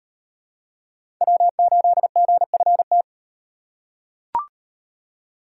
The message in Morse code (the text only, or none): W8GFT